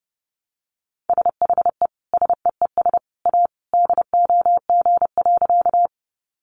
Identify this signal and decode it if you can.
Morse code, heard as a keyed tone, about 30 wpm, 700 Hz. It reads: SHE SEES A DOG.
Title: Easy Phrases 30wpm